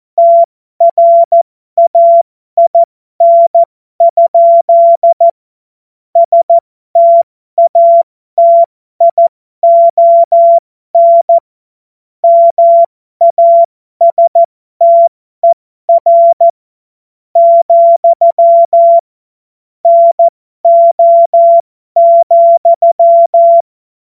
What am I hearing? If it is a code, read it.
Morse code, heard as a keyed tone, about 14 wpm, spaced as for 12 wpm, 680 Hz. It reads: TRAIN? STATION MASTER, NO,
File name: Morsekurs / Witze